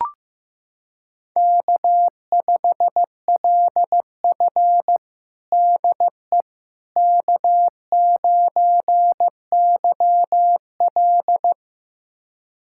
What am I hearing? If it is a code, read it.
Morse code, heard as a keyed tone, about 15 wpm, 700 Hz. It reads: K5LF DE K9YL
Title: USA Callsign DE Callsign - 4 Characters - 15wpm